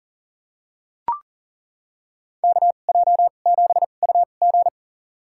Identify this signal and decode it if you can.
Morse code, heard as a keyed tone, about 40 wpm, spaced as for 29 wpm, 700 Hz. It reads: KJ7UG
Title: USA Call Signs - 2x Character Spacing - 40wpm